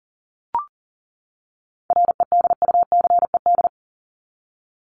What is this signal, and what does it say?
Morse code, heard as a keyed tone, about 40 wpm, 700 Hz. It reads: REDUCED
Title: ED Suffix - Top 100 Words with Base Words 40wpm